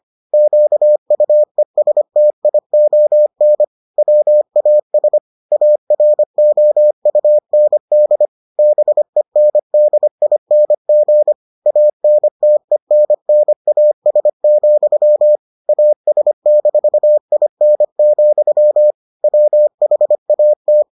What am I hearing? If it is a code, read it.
Morse code, heard as a keyed tone, about 25 wpm, 600 Hz. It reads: QUESTION WAS AROUND BENDING ANTENNAS, AS-IN, WHAT